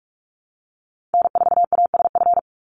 Morse code, heard as a keyed tone, about 44 wpm, 700 Hz. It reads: N4ASF
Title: Call Signs Repeated 3x - 45wpm